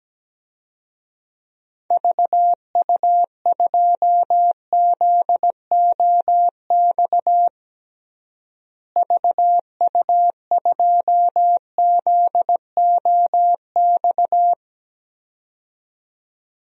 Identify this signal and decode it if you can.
Morse code, heard as a keyed tone, about 17 wpm, 700 Hz. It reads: VU2ZOX VU2ZOX
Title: DXCC-100 Call Signs - 6 Characters - Repeated 2x - 17wpm